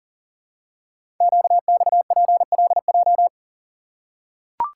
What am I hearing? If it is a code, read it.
Morse code, heard as a keyed tone, about 41 wpm, 700 Hz. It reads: QXPLJ
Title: Code Groups - 5 Letters - 40wpm